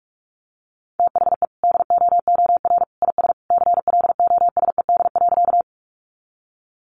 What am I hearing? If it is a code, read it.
Morse code, heard as a keyed tone, about 44 wpm, 700 Hz. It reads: THE DOOR IS CLOSED.
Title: Sentences from Top 1400 Words 45wpm